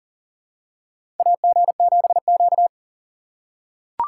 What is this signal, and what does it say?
Morse code, heard as a keyed tone, about 40 wpm, 700 Hz. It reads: AG7Q
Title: USA Call Signs - 3x Character Spacing - 40wpm